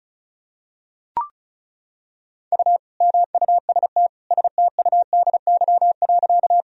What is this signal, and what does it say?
Morse code, heard as a keyed tone, about 36 wpm, 700 Hz. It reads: U MUST STUDY.
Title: Sentences from Top 300 Words and Abbreviations 35wpm